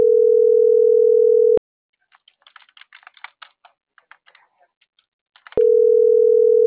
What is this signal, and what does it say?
Ringback tone (North America), 440 + 480 Hz.
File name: CENTRAL MABE / out mariel.reyes